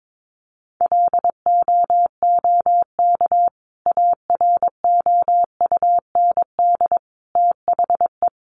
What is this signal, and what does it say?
Morse code, heard as a keyed tone, about 22 wpm, 700 Hz. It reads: LOOK AROUND THE
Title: Sentences from Top 600 Words 22wpm